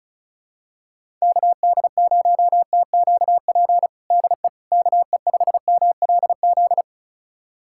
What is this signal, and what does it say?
Morse code, heard as a keyed tone, about 36 wpm, 700 Hz. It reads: KD0TQP DE KE5MLZ
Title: USA Callsign DE Callsign - 6 Characters - 35wpm